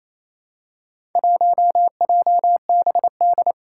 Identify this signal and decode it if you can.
Morse code, heard as a keyed tone, about 27 wpm, 700 Hz. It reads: 1JBD
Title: Four Challenging Letter-Number Combinations 28wpm